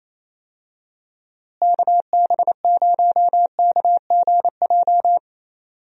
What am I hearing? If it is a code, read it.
Morse code, heard as a keyed tone, about 28 wpm, 700 Hz. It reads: KB0KGJ
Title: USA Call Signs - 6 Characters - Repeated 3x - 28wpm